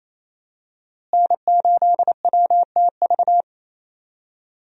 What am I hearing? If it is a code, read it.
Morse code, heard as a keyed tone, about 28 wpm, 700 Hz. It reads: N8WTV
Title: USA Call Signs - 5 Characters - Repeated 2x - 28wpm